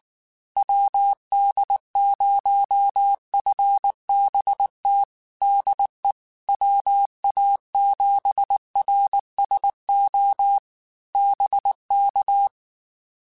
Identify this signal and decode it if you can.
Morse code, heard as a keyed tone, about 19 wpm, 800 Hz. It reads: WD0FBT DE WA7RSO BK
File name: QSOs / QSOs-mp3s / WD0FBT 1@18